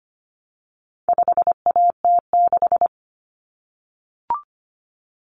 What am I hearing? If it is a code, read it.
Morse code, heard as a keyed tone, about 25 wpm, 700 Hz. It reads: 5AT6
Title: Common Cut Numbers - 4-Digit 25wpm